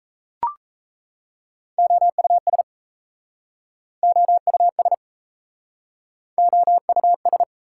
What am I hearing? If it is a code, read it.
Morse code, heard as a keyed tone, about 38 wpm, 690 Hz. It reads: OUS OUS OUS